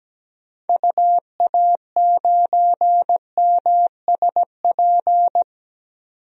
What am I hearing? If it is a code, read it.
Morse code, heard as a keyed tone, about 17 wpm, 700 Hz. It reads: UA9MSP
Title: DXCC-100 Call Signs - 6 Characters - 17wpm